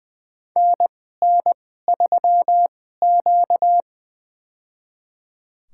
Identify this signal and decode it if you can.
Morse code, heard as a keyed tone, about 20 wpm, spaced as for 14 wpm, 700 Hz. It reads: NN3Q
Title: USA Call Signs - 4 Characters - 2x Character Spacing - 20wpm